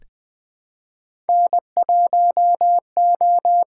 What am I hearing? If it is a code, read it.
Morse code, heard as a keyed tone, about 20 wpm, 700 Hz. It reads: N1O
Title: USA Call Signs - 5 Characters - 6x Character Spacing - 20wpm